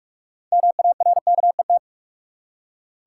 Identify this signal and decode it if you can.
Morse code, heard as a keyed tone, about 45 wpm, 700 Hz. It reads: MARKET
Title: Six Letter Words - 45wpm